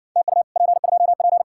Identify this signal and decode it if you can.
Morse code, heard as a keyed tone, about 60 wpm, 700 Hz. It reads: TU P2P